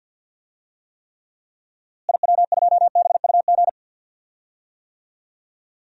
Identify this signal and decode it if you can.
Morse code, heard as a keyed tone, about 53 wpm, 700 Hz. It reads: IW2BUG
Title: DXCC-100 Call Signs - 6 Characters - Repeated 2x - 50wpm